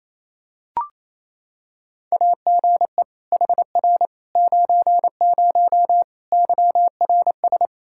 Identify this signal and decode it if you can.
Morse code, heard as a keyed tone, about 27 wpm, 700 Hz. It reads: AGE HR 90 YRS
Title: K1USN SST QSO Practice 28wpm